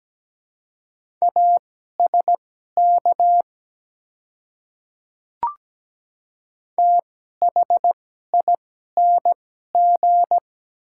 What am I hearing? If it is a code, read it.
Morse code, heard as a keyed tone, about 17 wpm, spaced as for 12 wpm, 700 Hz. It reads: ASK THING
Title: 2x Character Spacing - Top 200 Words - 17wpm